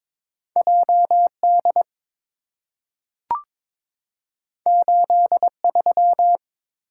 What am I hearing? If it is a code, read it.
Morse code, heard as a keyed tone, about 23 wpm, 700 Hz. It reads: JD 83
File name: practice / 22wpm / Two Letter-Number Combinations - 22wpm